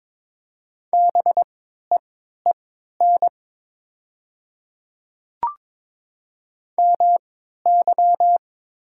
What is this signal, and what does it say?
Morse code, heard as a keyed tone, about 22 wpm, spaced as for 13 wpm, 700 Hz. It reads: BEEN MY
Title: 3x Character Spacing - Top 100 Words - 22wpm